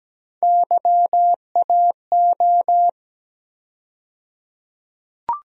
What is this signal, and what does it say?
Morse code, heard as a keyed tone, about 17 wpm, 700 Hz. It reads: YAO
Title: USA Call Sign Postfixes - 3 Letters 17wpm